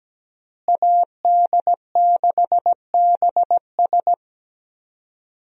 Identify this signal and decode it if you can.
Morse code, heard as a keyed tone, about 17 wpm, 700 Hz. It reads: AD6BS
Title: USA Call Signs - 5 Characters - Repeated 3x - 17wpm